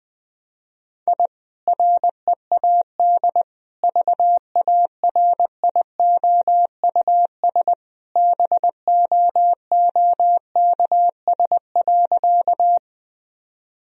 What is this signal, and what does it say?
Morse code, heard as a keyed tone, about 20 wpm, 700 Hz. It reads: I READ VARIOUS BOOKS.
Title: Sentences from Top 1000 Words - Review Part 4 - 20wpm